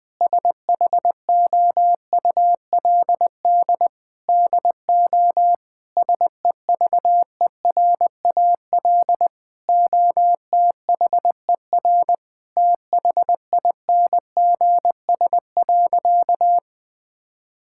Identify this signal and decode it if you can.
Morse code, heard as a keyed tone, about 20 wpm, 700 Hz. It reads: SHOULD DO SEVERAL OTHER THINGS.
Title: Sentences from Top 1000 Words - Review Part 6 - 20wpm